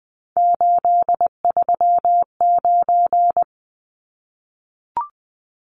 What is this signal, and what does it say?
Morse code, heard as a keyed tone, about 20 wpm, 700 Hz. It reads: 839
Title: Numbers - 3-Digit 20wpm